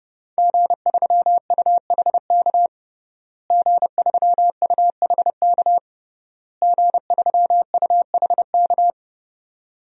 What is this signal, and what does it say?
Morse code, heard as a keyed tone, about 30 wpm, 700 Hz. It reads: G3UHK G3UHK G3UHK
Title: DXCC-100 Call Signs - 5 Characters - Repeated 3x - 30wpm